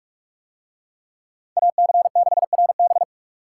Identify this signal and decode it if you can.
Morse code, heard as a keyed tone, about 47 wpm, 700 Hz. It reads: AC6RB